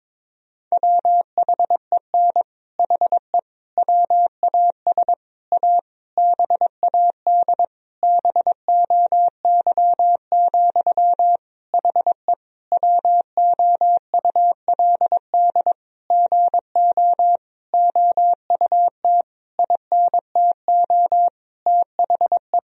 Morse code, heard as a keyed tone, about 22 wpm, 700 Hz. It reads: WHEN HE WAS A BAD BOY, HE WOULD GO OUT INTO THE
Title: Sentences from Top 1000 Words - Review Part 3 - 22wpm